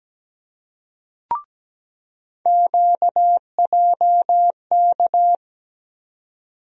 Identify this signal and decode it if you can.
Morse code, heard as a keyed tone, about 17 wpm, 700 Hz. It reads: QJK
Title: USA Call Sign Postfixes - 3 Letters 17wpm